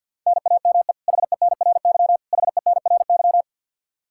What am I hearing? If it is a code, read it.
Morse code, heard as a keyed tone, about 52 wpm, 700 Hz. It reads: NAME HENRY HENRY
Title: Practice QSO - Exchange 1 - Beginning - 50wpm